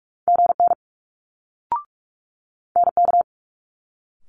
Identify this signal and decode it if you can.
Morse code, heard as a keyed tone, about 46 wpm, 700 Hz. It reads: GN NK
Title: Multi-letter Phonemes Quick Fire 45wpm